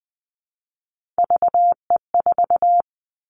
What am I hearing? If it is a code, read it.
Morse code, heard as a keyed tone, about 20 wpm, 700 Hz. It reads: VE4
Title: Callsign Prefixes for Top 50 DXCC - 20wpm